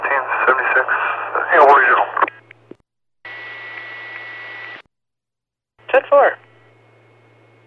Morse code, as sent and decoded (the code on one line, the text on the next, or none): none